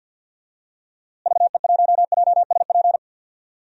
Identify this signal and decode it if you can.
Morse code, heard as a keyed tone, about 52 wpm, 700 Hz. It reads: VE1JSP
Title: DXCC-100 Call Signs - 6 Characters - Repeated 2x - 50wpm